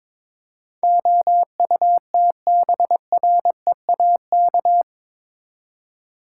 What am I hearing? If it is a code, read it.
Morse code, heard as a keyed tone, about 22 wpm, 700 Hz. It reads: OUTBREAK
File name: practice / 22wpm / Eight Letter Words - 22wpm